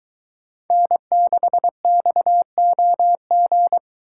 Morse code, heard as a keyed tone, about 22 wpm, 700 Hz. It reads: N6XOG